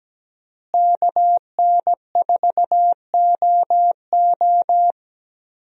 Morse code, heard as a keyed tone, about 17 wpm, 700 Hz. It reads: KN4OO
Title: USA Call Signs - 5 Characters - Repeated 3x - 17wpm